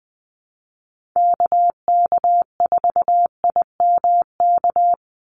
Morse code, heard as a keyed tone, about 20 wpm, 700 Hz. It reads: KK4IMK
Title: USA Call Signs 20wpm